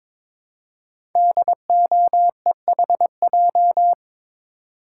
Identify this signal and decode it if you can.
Morse code, heard as a keyed tone, about 22 wpm, 700 Hz. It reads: DOEHJ
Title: Code Groups - 5 Letters - 22wpm